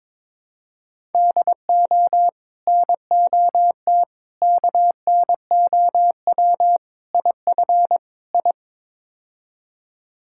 Morse code, heard as a keyed tone, about 22 wpm, 700 Hz. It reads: DO NOT KNOW IF I
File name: practice / 22wpm / Build A Phrase - 5-Word Phrases - 22wpm